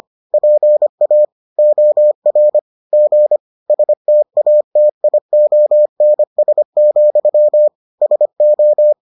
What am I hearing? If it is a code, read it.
Morse code, heard as a keyed tone, about 25 wpm, 600 Hz. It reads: PA OR G STATIONS, SO